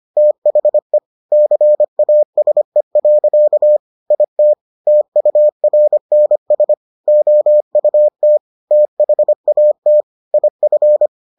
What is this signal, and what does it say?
Morse code, heard as a keyed tone, about 25 wpm, 600 Hz. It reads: THE CASE. IT TURNS OUT THAT IF